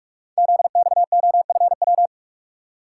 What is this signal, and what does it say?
Morse code, heard as a keyed tone, about 44 wpm, 700 Hz. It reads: ZXOFW
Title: Code Groups - 5 Letter:Numbers - 45wpm